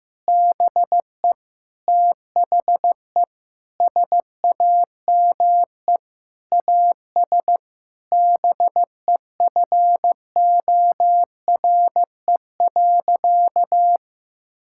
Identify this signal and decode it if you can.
Morse code, heard as a keyed tone, about 15 wpm, 700 Hz. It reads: BE THE SAME AS BEFORE.
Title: Sentences from Top 1000 Words - Review Part 5 - 15wpm